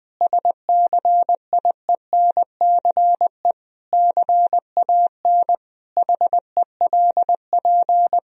Morse code, heard as a keyed tone, about 21 wpm, 700 Hz. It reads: SCIENCE CAN HELP
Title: Sentences from Top 1000 Words - Review Part 2 - 20wpm